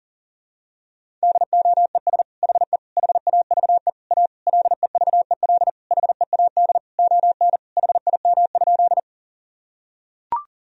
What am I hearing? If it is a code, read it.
Morse code, heard as a keyed tone, about 40 wpm, 700 Hz. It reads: DOES HE HAVE A LEVEL HEAD ON HIM?
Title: Sentences from Top 1000 Words - Review Part 4 - 40wpm